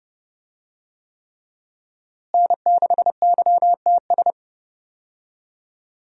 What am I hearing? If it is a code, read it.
Morse code, heard as a keyed tone, about 30 wpm, 700 Hz. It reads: N6YTS
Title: USA Call Signs - 5 Characters - Repeated 3x - 30wpm